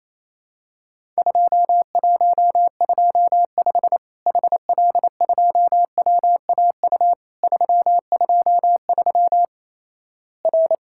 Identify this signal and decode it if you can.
Morse code, heard as a keyed tone, about 28 wpm, 700 Hz. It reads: 2125 HL2WAU 323 R
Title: Worked All Europe DX Contest - QTC Exchange - 28wpm